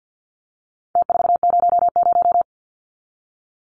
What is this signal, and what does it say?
Morse code, heard as a keyed tone, about 50 wpm, 700 Hz. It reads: T400
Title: Numbers - Contest Style - 4-Digit 50wpm